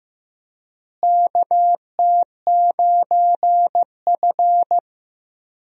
Morse code, heard as a keyed tone, about 15 wpm, 700 Hz. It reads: KT9F
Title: USA Call Signs - 4 Characters - Repeated 2x - 15wpm